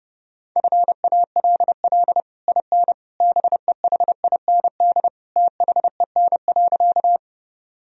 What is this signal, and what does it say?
Morse code, heard as a keyed tone, about 30 wpm, 700 Hz. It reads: FALL IN BEHIND THEN.
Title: Sentences from Top 1000 Words - Review Part 2 - 30wpm